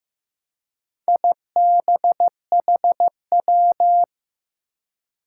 Morse code, heard as a keyed tone, about 15 wpm, 700 Hz. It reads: IBHW
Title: Four Challenging Letter-Number Combinations 15wpm